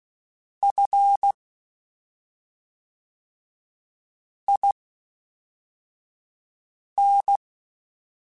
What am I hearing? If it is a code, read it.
Morse code, heard as a keyed tone, about 16 wpm, spaced as for 3 wpm, 800 Hz. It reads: FIN
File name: CW Comparisons / CW Comparisons-mp3s / CW Comparisons Broken Characters @5